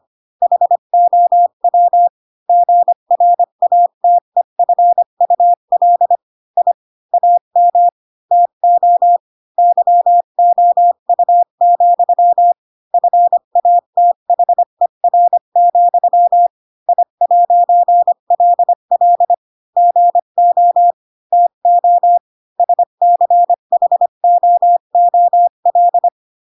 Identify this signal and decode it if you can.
Morse code, heard as a keyed tone, about 25 wpm, 700 Hz. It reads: HOW GRATEFUL I AM TO YOU, FATHER, I'LL GO TO SCHOOL
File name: Pinocchio / Pinocchio ch8 20 25 ik1hge